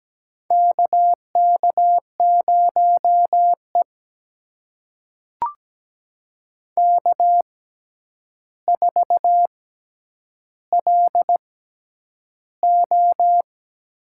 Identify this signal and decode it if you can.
Morse code, heard as a keyed tone, about 17 wpm, spaced as for 9 wpm, 700 Hz. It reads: KK0E K4LO
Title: USA Call Signs - 4 Characters - 6x Character Spacing - 17wpm